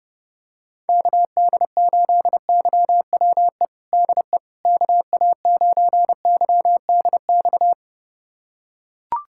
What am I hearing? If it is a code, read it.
Morse code, heard as a keyed tone, about 32 wpm, 700 Hz. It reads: KD8YWE DE KA9YDX
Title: USA Callsign DE Callsign 30wpm